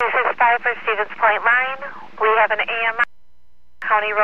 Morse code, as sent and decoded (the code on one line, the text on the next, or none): none